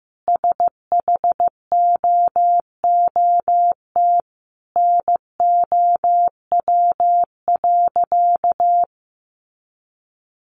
Morse code, heard as a keyed tone, about 15 wpm, 700 Hz. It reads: SHOOT NOW.
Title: Sentences from Top 1200 Words 15wpm